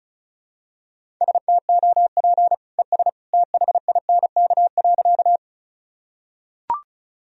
Morse code, heard as a keyed tone, about 36 wpm, 700 Hz. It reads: STOP ES THINK.
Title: Sentences from Top 300 Words and Abbreviations 35wpm